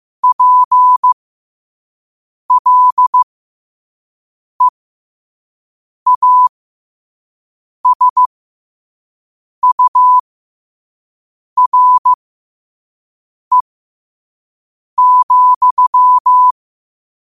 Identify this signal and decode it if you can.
Morse code, heard as a keyed tone, about 15 wpm, spaced as for 5 wpm, 1000 Hz. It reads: PLEASURE,
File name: qotdfiles / 5wpm